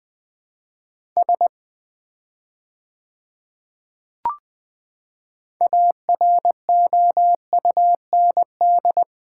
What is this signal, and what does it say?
Morse code, heard as a keyed tone, about 21 wpm, 700 Hz. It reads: S AROUND
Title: 4x Word-Spacing - Sets of 2 Words - 20wpm